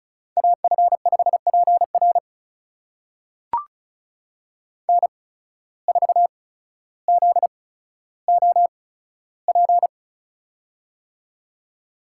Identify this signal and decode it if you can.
Morse code, heard as a keyed tone, about 34 wpm, spaced as for 15 wpm, 700 Hz. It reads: AF5PR N4ZOP
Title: USA Call Signs - 5 Characters - 8x Character Spacing - 35wpm